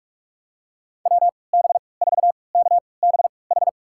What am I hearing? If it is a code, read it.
Morse code, heard as a keyed tone, about 46 wpm, spaced as for 26 wpm, 700 Hz. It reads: WB4KBH
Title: USA Call Signs - 3x Character Spacing - 45wpm